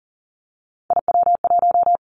Morse code, heard as a keyed tone, about 40 wpm, 700 Hz. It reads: IW1
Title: Callsign Prefixes for Top 50 DXCC - 40wpm